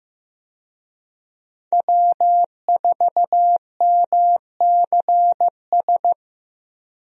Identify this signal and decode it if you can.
Morse code, heard as a keyed tone, about 15 wpm, 700 Hz. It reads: W4MCS
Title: USA Call Signs - 5 Characters - Repeated 3x - 15wpm